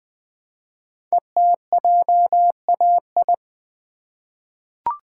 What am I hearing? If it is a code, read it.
Morse code, heard as a keyed tone, about 20 wpm, 700 Hz. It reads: ETJAI
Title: Code Groups - 5 Letters - 20wpm